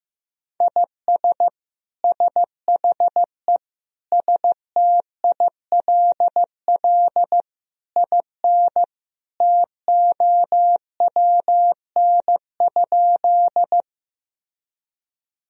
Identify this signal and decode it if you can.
Morse code, heard as a keyed tone, about 15 wpm, 700 Hz. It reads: IS SHE STILL IN TOWN?
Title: Sentences from Top 1000 Words - Review Part 2 - 15wpm